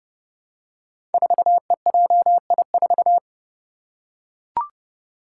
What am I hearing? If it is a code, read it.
Morse code, heard as a keyed tone, about 30 wpm, 700 Hz. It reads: 4EJI4
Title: Five Challenging Letter-Number Combinations 30wpm